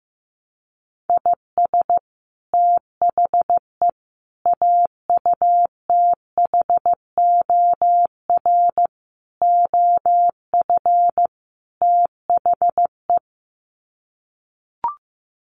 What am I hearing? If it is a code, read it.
Morse code, heard as a keyed tone, about 15 wpm, 700 Hz. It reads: IS THE AUTHOR OF THE
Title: Sets of 5 Words Encore 15wpm